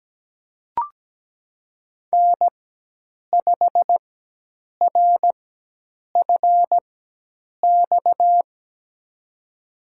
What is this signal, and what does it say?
Morse code, heard as a keyed tone, about 17 wpm, spaced as for 8 wpm, 700 Hz. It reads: N5RFX
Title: USA Call Signs - 4x Character Spacing - 17wpm